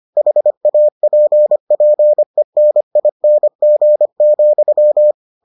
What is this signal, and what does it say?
Morse code, heard as a keyed tone, about 25 wpm, 600 Hz. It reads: HAPPENING,